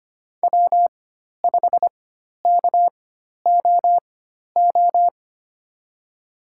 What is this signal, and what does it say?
Morse code, heard as a keyed tone, about 25 wpm, spaced as for 12 wpm, 700 Hz. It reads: W5KOO
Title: USA Call Signs - 5 Characters - 4x Character Spacing - 25wpm